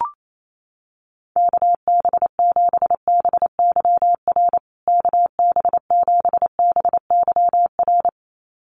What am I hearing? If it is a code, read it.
Morse code, heard as a keyed tone, about 28 wpm, 700 Hz. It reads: KB7BYR KB7BYR
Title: USA Call Signs Repeated 2x 28wpm